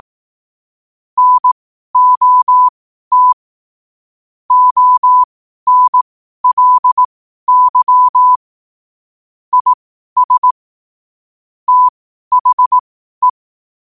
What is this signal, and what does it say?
Morse code, heard as a keyed tone, about 18 wpm, spaced as for 12 wpm, 1000 Hz. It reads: NOT ONLY IS THE
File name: qotdfiles / 12wpm